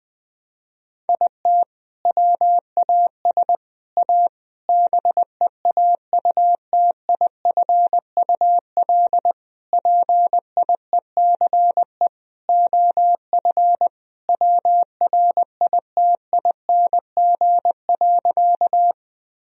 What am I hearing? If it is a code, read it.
Morse code, heard as a keyed tone, about 20 wpm, 700 Hz. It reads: IT WAS A BEAUTIFUL PIECE OF WRITING.
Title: Sentences from Top 1000 Words - Review Part 2 - 20wpm